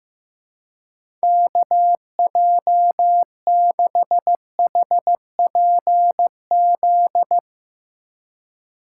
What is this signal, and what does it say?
Morse code, heard as a keyed tone, about 15 wpm, 700 Hz. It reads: KJ6HPZ
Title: USA Call Signs - 6 Characters - 2x Character Spacing - 15wpm